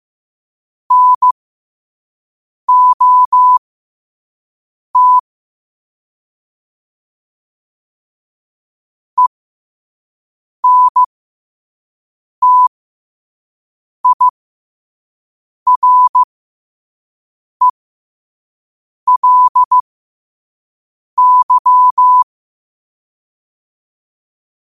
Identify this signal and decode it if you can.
Morse code, heard as a keyed tone, about 15 wpm, spaced as for 5 wpm, 1000 Hz. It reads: NOT ENTIRELY
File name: qotdfiles / 5wpm